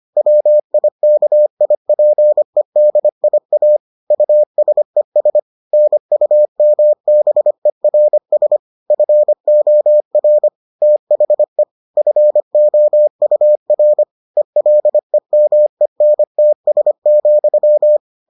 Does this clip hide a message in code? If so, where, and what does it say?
Yes: Morse code from 0.2 to 18.0 s, heard as a keyed tone, about 25 wpm, 600 Hz. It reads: WIKIPEDIA USES NUMBERS FOR THE FOUR ELEMENTS,